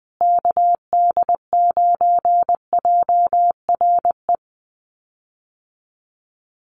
Morse code, heard as a keyed tone, about 20 wpm, 700 Hz. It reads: KD9JRE
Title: USA Call Signs Repeated 2x 20wpm